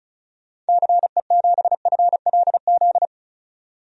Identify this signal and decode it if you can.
Morse code, heard as a keyed tone, about 36 wpm, 700 Hz. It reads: CE7FLZ
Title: DXCC-100 Call Signs - 6 Characters - 35wpm